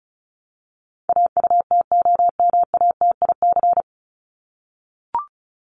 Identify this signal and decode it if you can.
Morse code, heard as a keyed tone, about 34 wpm, 700 Hz. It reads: AUTOMATIC
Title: Top 1751-2000 Words 35wpm